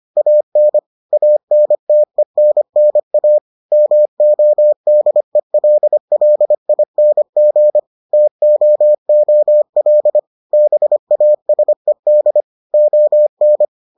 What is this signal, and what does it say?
Morse code, heard as a keyed tone, about 25 wpm, 600 Hz. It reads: AN ANTENNA MODELLING TOOL BASED ON